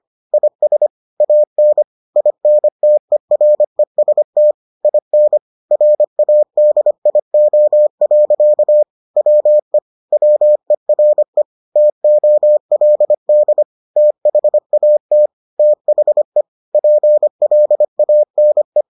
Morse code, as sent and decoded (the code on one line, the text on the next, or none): .. ... / .- -. / .. -. - . .-. ... - / .. -. / .-. .- -.. .. --- .-.-.- / .-- . / .-- . .-. / - --- .-.. -.. / - .... .- - / - .... / .--. .-.. .- -. .
IS AN INTEREST IN RADIO. WE WERE TOLD THAT THE PLANE